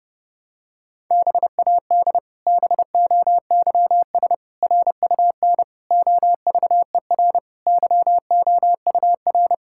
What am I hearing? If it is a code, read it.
Morse code, heard as a keyed tone, about 30 wpm, 700 Hz. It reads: BAD BOYS RUN OVER YOUR
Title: Pangram Challenge v2 - 30wpm